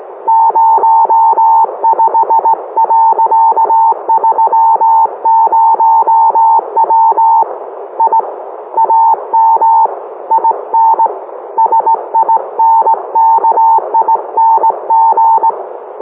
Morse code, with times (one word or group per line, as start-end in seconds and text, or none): 0.3-7.4 05.30W
8.0-8.2 I
8.8-9.9 AM
10.3-11.1 IN
11.6-15.5 SINKING